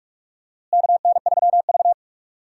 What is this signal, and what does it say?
Morse code, heard as a keyed tone, about 46 wpm, 700 Hz. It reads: KN3V
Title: USA Call Signs - 4 Characters - Repeated 3x - 45wpm